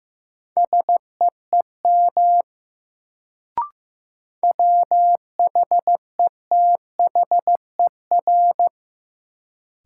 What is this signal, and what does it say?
Morse code, heard as a keyed tone, about 15 wpm, 700 Hz. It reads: SEEM WHETHER
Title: Sets of 3 Words - New Word Drill - 15wpm